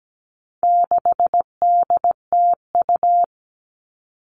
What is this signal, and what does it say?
Morse code, heard as a keyed tone, about 17 wpm, 700 Hz. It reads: 6DTU
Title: All Cut Numbers - 4-Digit 17wpm